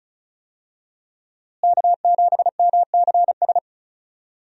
Morse code, heard as a keyed tone, about 36 wpm, 700 Hz. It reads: K7MCS
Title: USA Call Signs - 5 Characters - Repeated 3x - 35wpm